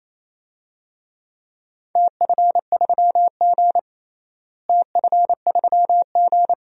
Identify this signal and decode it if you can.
Morse code, heard as a keyed tone, about 28 wpm, 700 Hz. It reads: TF3G TF3G
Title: DXCC-100 Call Signs - 4 Characters - Repeated 2x - 28wpm